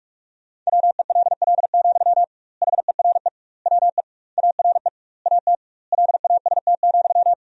Morse code, heard as a keyed tone, about 47 wpm, 700 Hz. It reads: WELL, HERE WE ARE AT LAST,
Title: 2x Word-Spacing - Sentences from Top 300 Words - 45wpm